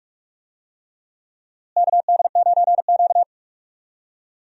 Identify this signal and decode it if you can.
Morse code, heard as a keyed tone, about 45 wpm, 700 Hz. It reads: KD9Q
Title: USA Call Signs - 4 Characters - Repeated 3x - 45wpm